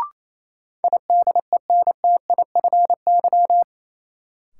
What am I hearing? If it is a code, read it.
Morse code, heard as a keyed tone, about 28 wpm, 700 Hz. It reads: IDENTIFY
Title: Sentences from Top 900 Words - New Word Drill - 28wpm